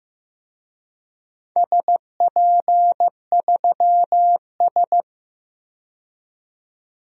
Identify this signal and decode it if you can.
Morse code, heard as a keyed tone, about 15 wpm, 700 Hz. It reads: SP3S
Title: DXCC-100 Call Signs - 4 Characters - 15wpm